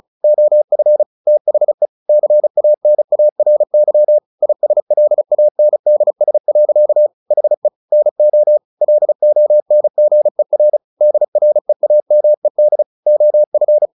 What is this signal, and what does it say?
Morse code, heard as a keyed tone, about 34 wpm, 600 Hz. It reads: OF THE CANARY ISLANDS. HE NO LONGER DREAMED OF
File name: the old man and the sea / the old man and the sea 1 35wpm